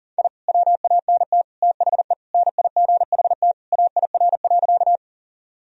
Morse code, heard as a keyed tone, about 40 wpm, 700 Hz. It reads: I WANT THE NIGHT AIR.
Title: Sentences from Top 1000 Words - Review Part 1 - 40wpm